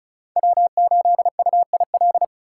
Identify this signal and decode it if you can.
Morse code, heard as a keyed tone, about 36 wpm, 700 Hz. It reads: W8UIL
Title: K1USN SST QSO Practice - 35wpm